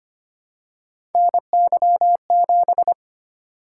Morse code, heard as a keyed tone, about 25 wpm, 700 Hz. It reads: NY7
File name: Three Letter-Number Combinations - 25wpm